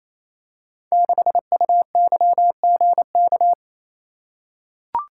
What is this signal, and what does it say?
Morse code, heard as a keyed tone, about 27 wpm, 700 Hz. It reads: 6UYGK
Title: Code Groups - 5 Letter:Numbers - 28wpm